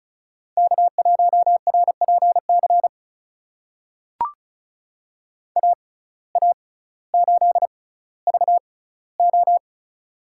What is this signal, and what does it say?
Morse code, heard as a keyed tone, about 36 wpm, spaced as for 18 wpm, 700 Hz. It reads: K1RPC AA8VO
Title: USA Call Signs - 5 Characters - 6x Character Spacing - 35wpm